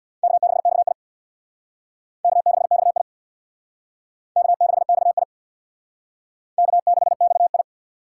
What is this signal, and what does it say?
Morse code, heard as a keyed tone, about 66 wpm, 700 Hz. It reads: K6XI K6XI K6XI K6XI